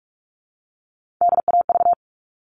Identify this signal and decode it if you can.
Morse code, heard as a keyed tone, about 44 wpm, 700 Hz. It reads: DAV
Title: All Cut Numbers - 3-Digit 45wpm